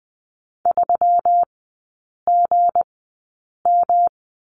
Morse code, heard as a keyed tone, about 20 wpm, spaced as for 8 wpm, 690 Hz. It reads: 3GM
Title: 191 Lesson 31e - Letter v - Three Characters 20wpm